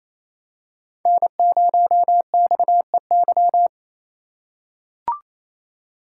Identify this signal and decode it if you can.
Morse code, heard as a keyed tone, about 28 wpm, 700 Hz. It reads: N0XEY
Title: USA Call Signs - 4x Character Spacing - 28wpm